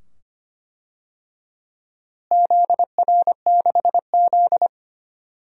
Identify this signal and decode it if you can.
Morse code, heard as a keyed tone, about 25 wpm, 700 Hz. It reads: ZR6Z
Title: DXCC-100 Call Signs - 4 Characters - Repeated 2x - 25wpm